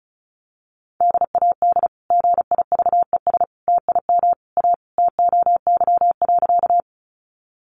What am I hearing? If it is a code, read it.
Morse code, heard as a keyed tone, about 34 wpm, 700 Hz. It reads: DAD GIVES TIM A TOY.
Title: Easy Phrases 35wpm